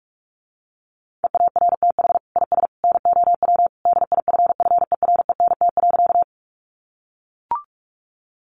Morse code, heard as a keyed tone, about 44 wpm, 700 Hz. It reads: EARTH IS NOW DIFFERENT.
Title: Sentences from Top 600 Words 45wpm